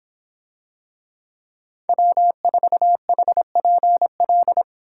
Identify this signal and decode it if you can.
Morse code, heard as a keyed tone, about 26 wpm, 700 Hz. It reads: W4HPL